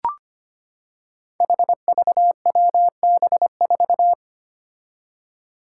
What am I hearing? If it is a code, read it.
Morse code, heard as a keyed tone, about 25 wpm, 700 Hz. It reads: HVWB4
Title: Five Challenging Letter-Number Combinations 25wpm